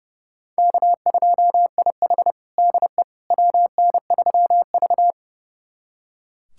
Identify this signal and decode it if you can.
Morse code, heard as a keyed tone, about 30 wpm, 700 Hz. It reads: K2IH DE WN3V
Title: USA Callsign DE Callsign - 4 Characters - 30wpm